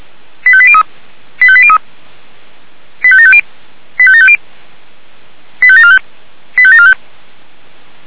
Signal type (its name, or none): five-tone selective call